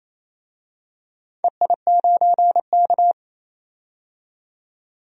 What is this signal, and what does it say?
Morse code, heard as a keyed tone, about 29 wpm, 690 Hz. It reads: EI9K
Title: DXCC-100 Call Signs - 4 Characters - 28wpm